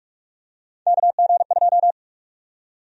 Morse code, heard as a keyed tone, about 46 wpm, 700 Hz. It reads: KG2
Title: USA Call Sign Prefixes - 45wpm